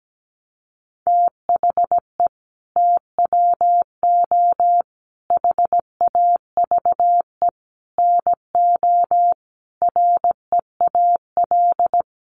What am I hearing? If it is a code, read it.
Morse code, heard as a keyed tone, about 17 wpm, 700 Hz. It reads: THE TWO HAVE NO REAL
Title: Sentences from Top 1300 Words 17wpm